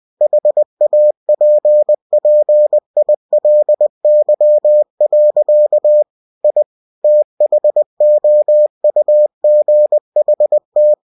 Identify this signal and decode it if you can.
Morse code, heard as a keyed tone, about 20 wpm, 600 Hz. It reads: HAPPILY. I THOUGHT
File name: cw-books / winnie-the-pooh / winnie-20wpm-page-0077